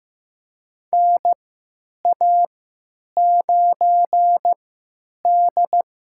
Morse code, heard as a keyed tone, about 15 wpm, spaced as for 9 wpm, 700 Hz. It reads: NA9D
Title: USA Call Signs - 4 Characters - 3x Character Spacing - 15wpm